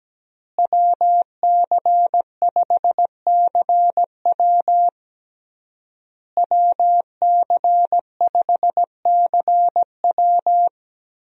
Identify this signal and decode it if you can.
Morse code, heard as a keyed tone, about 17 wpm, 700 Hz. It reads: WC5CW WC5CW
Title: USA Call Signs - 5 Characters - Repeated 3x - 17wpm